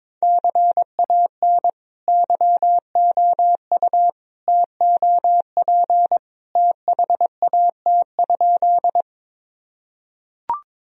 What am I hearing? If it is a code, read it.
Morse code, heard as a keyed tone, about 22 wpm, 700 Hz. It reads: CAN YOU TOP THAT?
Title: Sentences from Top 1000 Words - Review Part 2 - 22wpm